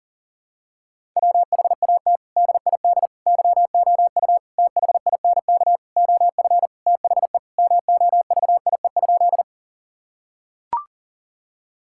Morse code, heard as a keyed tone, about 40 wpm, 700 Hz. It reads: WHAT DID YOU THINK OF THE MOVIE?
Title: Sentences from Top 1000 Words - Review Part 5 - 40wpm